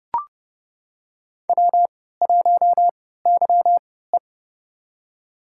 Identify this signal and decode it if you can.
Morse code, heard as a keyed tone, about 32 wpm, spaced as for 17 wpm, 700 Hz. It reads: W1YE
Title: USA Call Signs - 4 Characters - 3x Character Spacing - 30wpm